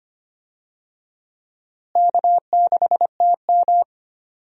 Morse code, heard as a keyed tone, about 25 wpm, 700 Hz. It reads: K6TM
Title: USA Call Signs - 4 Characters - Repeated 2x - 25wpm